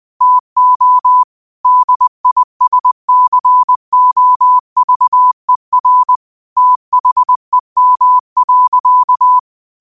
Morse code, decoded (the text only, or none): TO DISCOVER THEM.